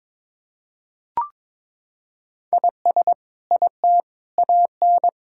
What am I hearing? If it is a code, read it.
Morse code, heard as a keyed tone, about 22 wpm, 700 Hz. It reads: IS IT AN
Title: Sentences from Top 1000 Words - Review Part 6 - 22wpm